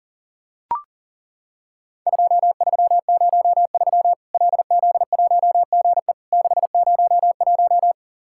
Morse code, heard as a keyed tone, about 40 wpm, 700 Hz. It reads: 2303 LZ1GE 601
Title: Worked All Europe DX Contest - QTC Exchange - 40wpm